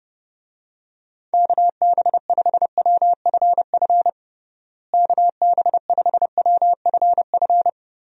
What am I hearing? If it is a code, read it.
Morse code, heard as a keyed tone, about 30 wpm, 700 Hz. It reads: KB5WFF KB5WFF
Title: USA Call Signs - 6 Characters - Repeated 3x - 30wpm